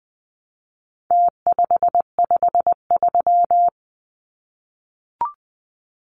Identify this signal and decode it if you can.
Morse code, heard as a keyed tone, about 20 wpm, 700 Hz. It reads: T553
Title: Numbers - Contest Style - 4-Digit 20wpm